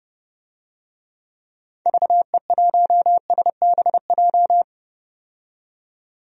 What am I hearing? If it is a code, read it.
Morse code, heard as a keyed tone, about 32 wpm, 700 Hz. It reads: VE1SBJ
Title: DXCC-100 Call Signs - 6 Characters - 30wpm